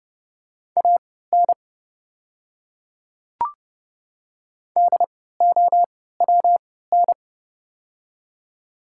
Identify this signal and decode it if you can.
Morse code, heard as a keyed tone, about 30 wpm, spaced as for 17 wpm, 700 Hz. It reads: AN DOWN
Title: 3x Character Spacing - Top 100 Words - 30wpm